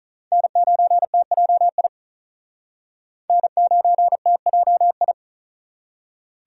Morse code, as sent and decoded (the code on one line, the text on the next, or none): -. ----. - .--- .. / -. ----. - .--- ..
N9TJI N9TJI